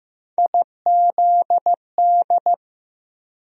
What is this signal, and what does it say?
Morse code, heard as a keyed tone, about 16 wpm, 700 Hz. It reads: IZD
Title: USA Call Sign Postfixes - 3 Letters 15wpm